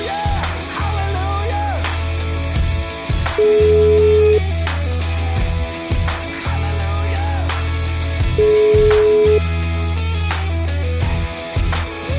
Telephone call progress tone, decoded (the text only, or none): ringback tone